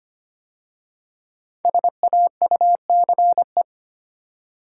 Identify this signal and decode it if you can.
Morse code, heard as a keyed tone, about 25 wpm, 700 Hz. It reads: SAUCE
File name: 5 Letter Words - First 3 Letters Given - 25wpm